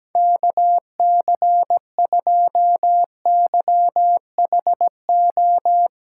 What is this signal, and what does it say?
Morse code, heard as a keyed tone, about 17 wpm, 700 Hz. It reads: KC2YHO
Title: USA Callsign DE Callsign - 6 Characters - 17wpm